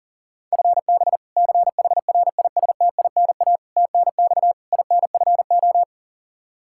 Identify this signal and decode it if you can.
Morse code, heard as a keyed tone, about 40 wpm, 700 Hz. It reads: FB CHRISTINA TNX INFO
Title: Practice QSO - Exchange 3 - More Info - 40wpm